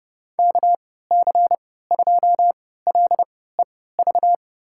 Morse code, heard as a keyed tone, about 30 wpm, spaced as for 17 wpm, 700 Hz. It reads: KC2LEV
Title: USA Call Signs - 6 Characters - 3x Character Spacing - 30wpm